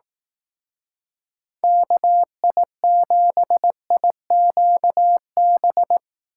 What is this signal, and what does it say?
Morse code, heard as a keyed tone, about 18 wpm, 700 Hz. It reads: KI7IQB
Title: Speed Racing - USA 2x3 Call Signs 15wpm